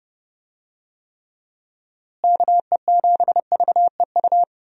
Morse code, heard as a keyed tone, about 30 wpm, 700 Hz. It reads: KE7VEU
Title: USA Call Signs - 6 Characters - Repeated 2x - 30wpm